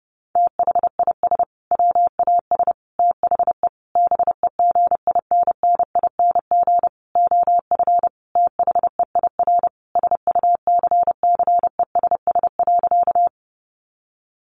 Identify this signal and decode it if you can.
Morse code, heard as a keyed tone, about 30 wpm, 700 Hz. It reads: THIS WAS THE BEGINNING OF THEIR SUCCESS.
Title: Sentences from Top 900 Words 30wpm